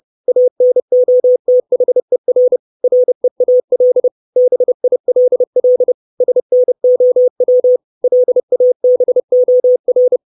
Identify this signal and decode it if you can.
Morse code, heard as a keyed tone, about 30 wpm, 500 Hz. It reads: ANOTHER REAL BILL SNOW LABOR
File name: 632 Words 30WPM